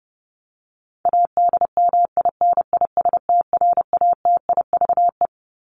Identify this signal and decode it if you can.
Morse code, heard as a keyed tone, about 30 wpm, 700 Hz. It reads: ADMINISTRATIVE
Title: Top 1501-1750 Words 30wpm